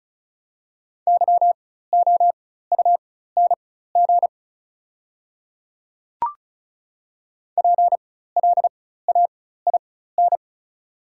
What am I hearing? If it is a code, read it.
Morse code, heard as a keyed tone, about 35 wpm, spaced as for 16 wpm, 690 Hz. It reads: YOUNG PLAIN
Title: 4x Character Spacing - Top 500 Words - 35wpm